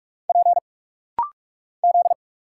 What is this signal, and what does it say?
Morse code, heard as a keyed tone, about 46 wpm, 700 Hz. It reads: P Z
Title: Single Letter-Number Alt - Rapid-Fire 45wpm